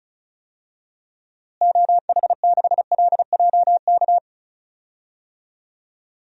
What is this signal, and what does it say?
Morse code, heard as a keyed tone, about 34 wpm, 700 Hz. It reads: OH6LJK
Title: DXCC-100 Call Signs - 6 Characters - Repeated 2x - 35wpm